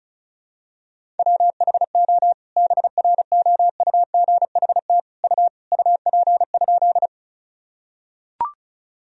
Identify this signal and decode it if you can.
Morse code, heard as a keyed tone, about 36 wpm, 700 Hz. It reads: WHO BROUGHT U UP?
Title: Sentences from Top 500 Words and Abbreviations 35wpm